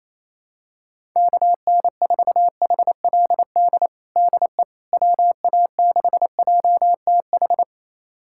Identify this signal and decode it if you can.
Morse code, heard as a keyed tone, about 28 wpm, 700 Hz. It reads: KN4HLD DE WA6JTH
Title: USA Callsign DE Callsign 28wpm